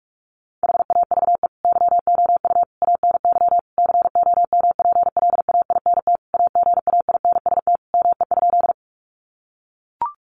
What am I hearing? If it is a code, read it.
Morse code, heard as a keyed tone, about 44 wpm, 700 Hz. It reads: HAVE YOU ANY COMPLAINT AGAINST ME?
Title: Sentences from Top 1400 Words 45wpm